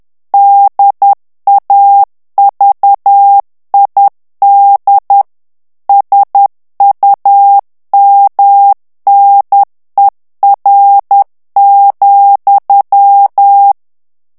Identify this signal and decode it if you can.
Morse code, heard as a keyed tone, about 10 wpm, 800 Hz. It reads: DAVID SUMNER,
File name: ANGLAIS texte clair 3 vitesse 10 mots minute